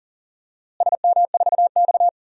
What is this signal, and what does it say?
Morse code, heard as a keyed tone, about 38 wpm, 700 Hz. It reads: SM4X
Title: DXCC-100 Call Signs - 4 Characters - Repeated 3x - 40wpm